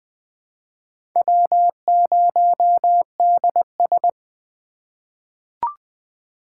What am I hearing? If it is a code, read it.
Morse code, heard as a keyed tone, about 20 wpm, 700 Hz. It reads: W0DS
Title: USA Call Signs - 4 Characters - 3x Character Spacing - 20wpm